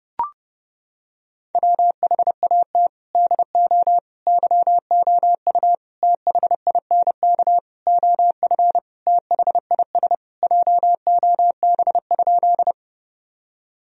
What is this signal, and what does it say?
Morse code, heard as a keyed tone, about 32 wpm, 700 Hz. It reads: WHAT DO YOU THINK OF THIS JOB?
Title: Sentences from Top 1000 Words - Review Part 3 - 30wpm